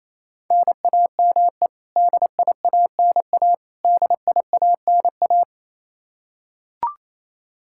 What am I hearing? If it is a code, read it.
Morse code, heard as a keyed tone, about 28 wpm, 700 Hz. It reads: NAME DIANA DIANA
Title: Practice QSO - Exchange 1 - Beginning - 28wpm